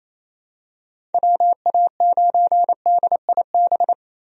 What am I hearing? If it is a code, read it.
Morse code, heard as a keyed tone, about 28 wpm, 700 Hz. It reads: WA9DIB
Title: USA Call Signs - 6 Characters - Repeated 2x - 28wpm